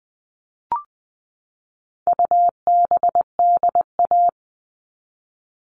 Morse code, heard as a keyed tone, about 20 wpm, 700 Hz. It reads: UBDA
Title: All Cut Numbers - 4-Digit 20wpm